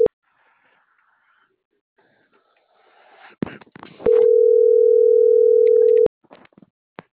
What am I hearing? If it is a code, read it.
Ringback tone (North America), 440 + 480 Hz.